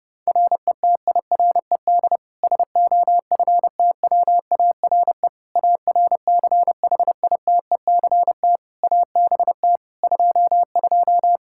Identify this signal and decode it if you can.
Morse code, heard as a keyed tone, about 30 wpm, 700 Hz. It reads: RETIRED SOFTWARE ARCHITECT ABT 22
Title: Practice QSO - Exchange 3 - More Info - 30wpm